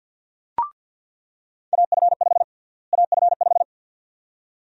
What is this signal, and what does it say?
Morse code, heard as a keyed tone, about 46 wpm, 700 Hz. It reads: AF5 AF5
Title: USA Call Sign Prefixes - Repeated 2x - 50wpm